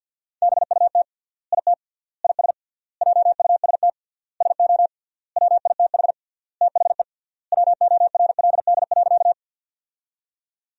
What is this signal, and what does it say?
Morse code, heard as a keyed tone, about 50 wpm, 700 Hz. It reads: BUT IT IS JUST SO WITH THE WORLD.
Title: 3x Word-Spacing - Sentences from Top 200 Words - 50wpm